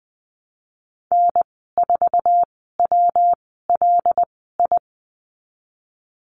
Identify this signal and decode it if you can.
Morse code, heard as a keyed tone, about 19 wpm, spaced as for 14 wpm, 700 Hz. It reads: N4WLI
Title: 122 Lesson 20g - Number 3 - Call Signs 20wpm